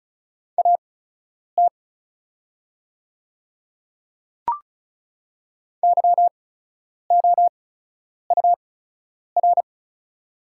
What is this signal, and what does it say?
Morse code, heard as a keyed tone, about 36 wpm, spaced as for 10 wpm, 700 Hz. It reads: AT YOUR
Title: 8x Character Spacing - Top 100 Words - 35wpm